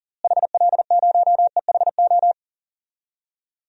Morse code, heard as a keyed tone, about 37 wpm, 700 Hz. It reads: HL0EHO
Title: DXCC-100 Call Signs - 6 Characters - Repeated 3x - 40wpm